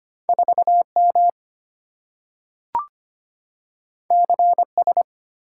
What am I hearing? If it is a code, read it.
Morse code, heard as a keyed tone, about 25 wpm, 690 Hz. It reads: 4M CS
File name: Two Letter-Number Combinations - 25wpm